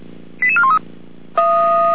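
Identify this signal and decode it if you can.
Five-tone selective call (ZVEI), heard as quick runs of five tones, tones of about 70 ms. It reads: 88413
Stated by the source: ZVEI five-tone selective call received via SDR.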